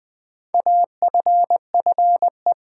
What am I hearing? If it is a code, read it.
Morse code, heard as a keyed tone, about 20 wpm, 700 Hz. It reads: AFFE
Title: Build A Word - 2 Letters At A Time - 6-Letter Words - 20wpm